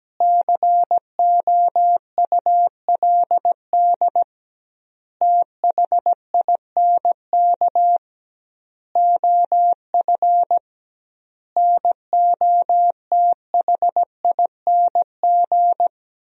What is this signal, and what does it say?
Morse code, heard as a keyed tone, about 17 wpm, 700 Hz. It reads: COULD THINK OF NOTHING
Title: 2x Word-Spacing - Sentences from Top 500 Words - 17wpm